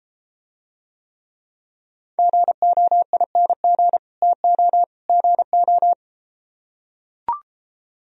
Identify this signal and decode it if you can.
Morse code, heard as a keyed tone, about 33 wpm, 700 Hz. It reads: GOING TO GO